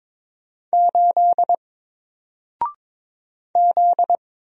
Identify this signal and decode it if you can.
Morse code, heard as a keyed tone, about 22 wpm, 700 Hz. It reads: 8 Z
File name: Single-Letter-Number-22wpm